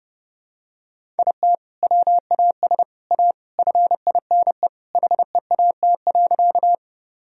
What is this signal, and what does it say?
Morse code, heard as a keyed tone, about 32 wpm, 700 Hz. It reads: IT WAS A FINE HEAT.
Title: Sentences from Top 1000 Words - Review Part 3 - 30wpm